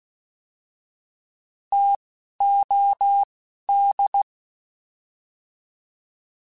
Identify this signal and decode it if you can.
Morse code, heard as a keyed tone, about 16 wpm, spaced as for 12 wpm, 800 Hz. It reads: TOD